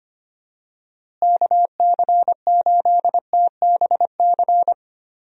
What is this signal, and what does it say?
Morse code, heard as a keyed tone, about 25 wpm, 700 Hz. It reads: KC8TBC
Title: USA Call Signs - 6 Characters - Repeated 2x - 25wpm